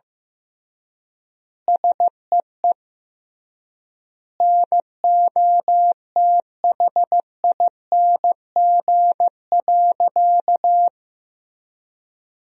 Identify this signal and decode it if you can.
Morse code, heard as a keyed tone, about 16 wpm, 700 Hz. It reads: SEE NOTHING.
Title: 3x Word-Spacing - Sentences from Top 500 Words - 15wpm